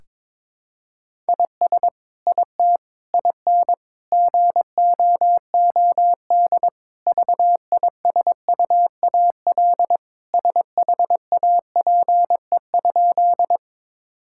Morse code, heard as a keyed tone, about 22 wpm, 700 Hz. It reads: IS IT IN GOOD VISUAL SHAPE?
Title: Sentences from Top 1000 Words - Review Part 6 - 22wpm